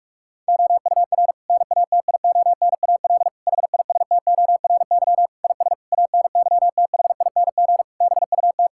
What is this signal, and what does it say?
Morse code, heard as a keyed tone, about 41 wpm, 700 Hz. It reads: OUR NATIONAL HISTORY IS ANYTHING BUT
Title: Sentences from Top 1000 Words - Review Part 6 - 45wpm